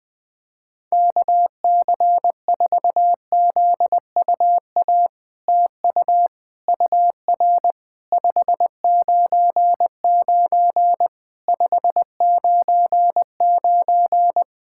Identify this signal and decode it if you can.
Morse code, heard as a keyed tone, about 21 wpm, 700 Hz. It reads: KC4ZUA TU UR 599 599
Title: POTA QSO Practice - 20wpm